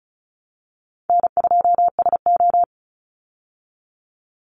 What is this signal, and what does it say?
Morse code, heard as a keyed tone, about 34 wpm, 700 Hz. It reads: N2SO
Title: Easy Call Signs Repeated 2x - 35wpm